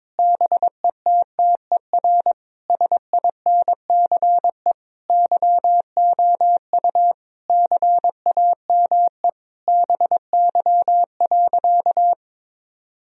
Morse code, heard as a keyed tone, about 22 wpm, 700 Hz. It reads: BETTER SINCE YOU CAME BY.
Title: Sentences from Top 1000 Words - Review Part 1 - 22wpm